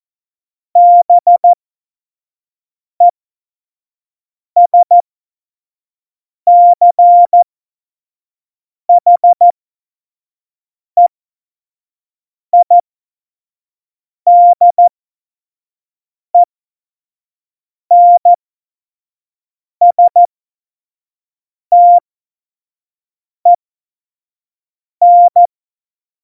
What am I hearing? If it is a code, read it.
Morse code, heard as a keyed tone, about 14 wpm, spaced as for 5 wpm, 700 Hz. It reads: BESCHEIDENSTEN
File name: Morsekurs / Witze / Witz 25. Dezember 2018, 25 F70